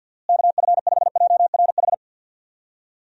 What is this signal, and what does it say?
Morse code, heard as a keyed tone, about 46 wpm, 700 Hz. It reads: KF5JRH
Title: USA Call Signs - 6 Characters - Repeated 2x - 50wpm